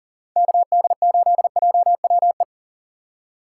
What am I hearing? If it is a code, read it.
Morse code, heard as a keyed tone, about 40 wpm, 700 Hz. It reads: KD8JWE
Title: USA Call Signs - 2x Character Spacing - 40wpm